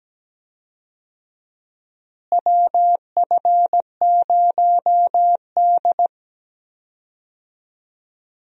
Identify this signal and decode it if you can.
Morse code, heard as a keyed tone, about 17 wpm, 700 Hz. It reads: WF0D
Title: USA Call Signs - 4 Characters - Repeated 3x - 17wpm